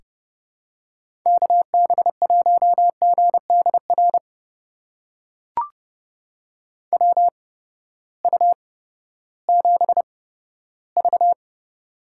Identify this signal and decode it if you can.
Morse code, heard as a keyed tone, about 30 wpm, spaced as for 15 wpm, 700 Hz. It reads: KB1GDR WU7V
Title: USA Call Signs - 8x Character Spacing - 30wpm